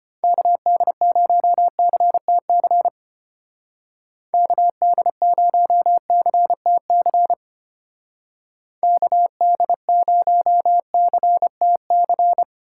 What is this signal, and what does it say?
Morse code, heard as a keyed tone, about 30 wpm, 700 Hz. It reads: KD0CTC KD0CTC KD0CTC
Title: Speed Racing - USA 2x3 Call Signs 25wpm